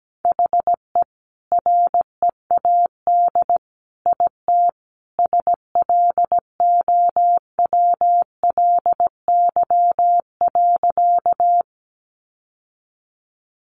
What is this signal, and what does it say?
Morse code, heard as a keyed tone, about 17 wpm, 700 Hz. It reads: HE READ IT SLOWLY.
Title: Sentences from Top 1100 Words 17wpm